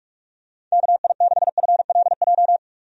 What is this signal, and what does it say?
Morse code, heard as a keyed tone, about 46 wpm, 700 Hz. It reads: KI6FLJ
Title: USA Call Signs - 6 Characters - 45wpm